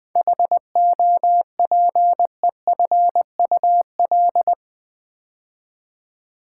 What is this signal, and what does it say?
Morse code, heard as a keyed tone, about 20 wpm, 700 Hz. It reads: HOPEFUL
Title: 7 Letter Words - First 3 Letters Given - 20wpm